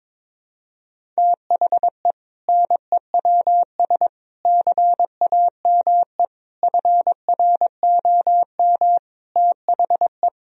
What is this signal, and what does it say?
Morse code, heard as a keyed tone, about 22 wpm, 700 Hz. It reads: THE NEWS CAME FROM THE
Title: Sentences from Top 1000 Words - Review Part 1 - 22wpm